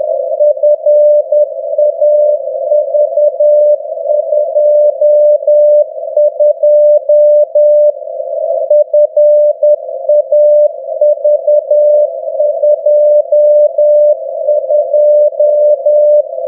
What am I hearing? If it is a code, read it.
Morse code, heard as a keyed tone, about 10 wpm, 600 Hz. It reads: FAV22 FAV22
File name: europe / f-2936-vvv de fav22-2015-01-k-rb